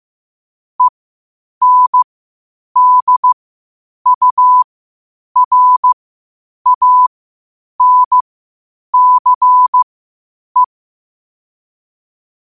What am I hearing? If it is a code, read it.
Morse code, heard as a keyed tone, about 15 wpm, spaced as for 8 wpm, 1000 Hz. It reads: ENDURANCE
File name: qotdfiles / 8wpm